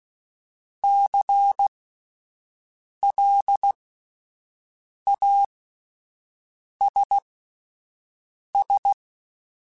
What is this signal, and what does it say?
Morse code, heard as a keyed tone, about 16 wpm, spaced as for 5 wpm, 800 Hz. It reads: CLASS
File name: (spaced) / QSOs Spaced-mp3s / WA7QOX 2 @10 sp